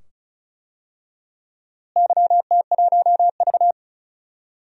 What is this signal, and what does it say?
Morse code, heard as a keyed tone, about 36 wpm, 700 Hz. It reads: YT1V
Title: DXCC-100 Call Signs - 4 Characters - Repeated 2x - 35wpm